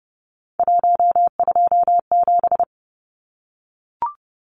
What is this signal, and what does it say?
Morse code, heard as a keyed tone, about 30 wpm, 700 Hz. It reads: 127
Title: Numbers - 3-Digit 30wpm